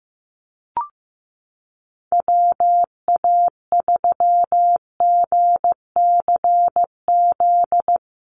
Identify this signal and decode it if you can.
Morse code, heard as a keyed tone, about 15 wpm, 700 Hz. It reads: WA3GCZ
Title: USA Call Signs Repeated 3x 15wpm